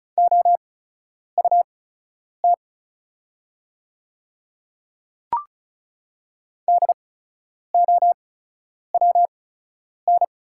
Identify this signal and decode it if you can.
Morse code, heard as a keyed tone, about 35 wpm, spaced as for 10 wpm, 690 Hz. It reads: OUT DOWN
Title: 8x Character Spacing - Top 300 Words - 35wpm